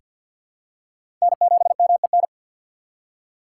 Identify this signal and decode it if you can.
Morse code, heard as a keyed tone, about 46 wpm, 700 Hz. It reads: N7MEN
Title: USA Call Signs - 5 Characters - 2x Character Spacing - 50wpm